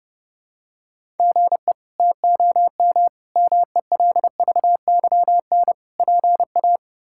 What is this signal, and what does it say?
Morse code, heard as a keyed tone, about 30 wpm, 700 Hz. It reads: GE TOM MELVYN PA
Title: K1USN SST QSO Practice - 30wpm